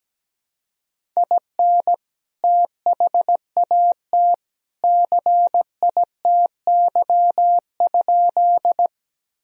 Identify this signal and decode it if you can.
Morse code, heard as a keyed tone, about 17 wpm, 700 Hz. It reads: IN THAT CITY?
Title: Sentences from Top 1000 Words - Review Part 1 - 17wpm